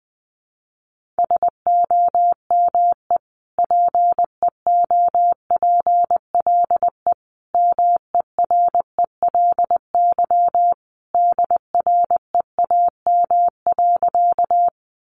Morse code, heard as a keyed tone, about 20 wpm, 700 Hz. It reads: SOME PEOPLE MERELY DREAM.
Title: Sentences from Top 1500 Words 20wpm